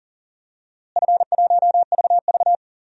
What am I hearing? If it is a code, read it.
Morse code, heard as a keyed tone, about 40 wpm, 700 Hz. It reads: F1VV
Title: DXCC-100 Call Signs - 4 Characters - Repeated 3x - 40wpm